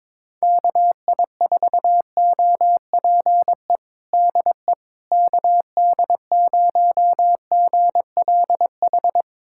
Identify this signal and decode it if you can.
Morse code, heard as a keyed tone, about 21 wpm, 700 Hz. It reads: KI4OPE DE KD0GLH
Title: USA Callsign DE Callsign 22wpm